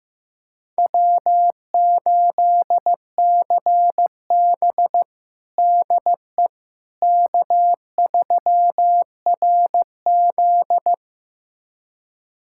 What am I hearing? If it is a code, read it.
Morse code, heard as a keyed tone, about 15 wpm, 700 Hz. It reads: W8CB DE K3RZ
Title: USA Callsign DE Callsign - 4 Characters - 15wpm